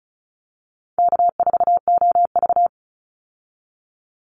Morse code, heard as a keyed tone, about 34 wpm, 700 Hz. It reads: K4OV
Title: Easy Call Signs Repeated 3x - 35wpm